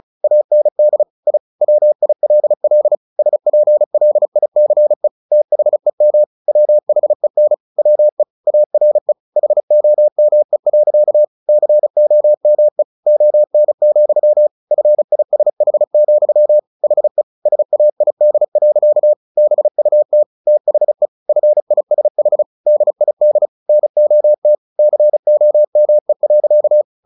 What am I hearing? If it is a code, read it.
Morse code, heard as a keyed tone, about 34 wpm, 600 Hz. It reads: AND I WILL SPLICE THEM WHEN WE ARE HOME. COME ON, FISH, HE SAID. BUT THE FISH DID NOT COME.
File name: the old man and the sea 2 35wpm